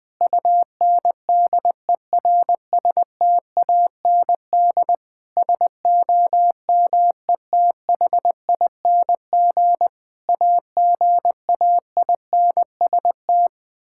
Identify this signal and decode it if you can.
Morse code, heard as a keyed tone, about 21 wpm, 700 Hz. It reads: UNDERSTAND SOMETHING AGAINST
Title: Sentences with 9-Word Phrases - Spoken Beginning - 20wpm